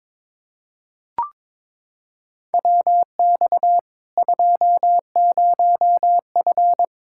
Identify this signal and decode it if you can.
Morse code, heard as a keyed tone, about 22 wpm, 700 Hz. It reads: WX 20F
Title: K1USN SST QSO Practice 22wpm